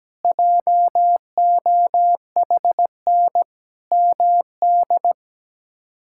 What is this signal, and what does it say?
Morse code, heard as a keyed tone, about 17 wpm, 700 Hz. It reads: JOHN MD
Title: K1USN SST QSO Practice - 17wpm